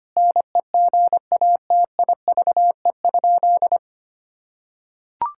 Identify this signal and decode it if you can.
Morse code, heard as a keyed tone, about 25 wpm, 700 Hz. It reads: NEGATIVE?
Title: Sentences from Top 1000 Words - Review Part 6 - 25wpm